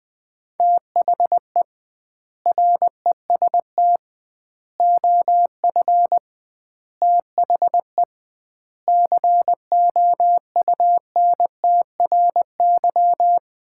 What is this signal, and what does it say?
Morse code, heard as a keyed tone, about 20 wpm, 700 Hz. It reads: THE REST OF THE COUNTRY
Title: 2x Word-Spacing - Sets of 5 Words - 20wpm